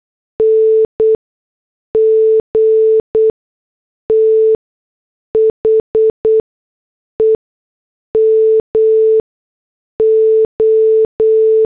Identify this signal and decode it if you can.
Morse code, heard as a keyed tone, about 8 wpm, spaced as for 6 wpm, 440 Hz. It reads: NG THE MO
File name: morse / headlines